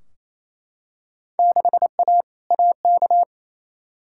Morse code, heard as a keyed tone, about 28 wpm, 700 Hz. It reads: 6A AK
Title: Field Day Exchanges Encore - 28wpm